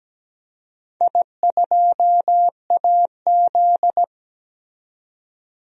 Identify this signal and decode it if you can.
Morse code, heard as a keyed tone, about 17 wpm, 700 Hz. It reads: I2AZ
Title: DXCC-100 Call Signs - 4 Characters - 17wpm